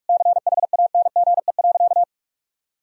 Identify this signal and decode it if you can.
Morse code, heard as a keyed tone, about 46 wpm, 700 Hz. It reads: CHANGE.
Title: Sentences from Top 1000 Words - Review Part 1 - 45wpm